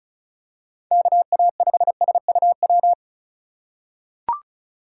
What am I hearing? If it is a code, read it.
Morse code, heard as a keyed tone, about 34 wpm, 700 Hz. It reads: KA5SUW
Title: USA Call Signs - 2x Character Spacing - 35wpm